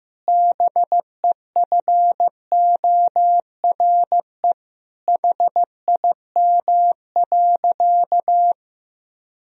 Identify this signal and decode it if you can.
Morse code, heard as a keyed tone, about 15 wpm, 700 Hz. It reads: BEFORE HIM.
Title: Sentences from Top 1000 Words - Review Part 2 - 15wpm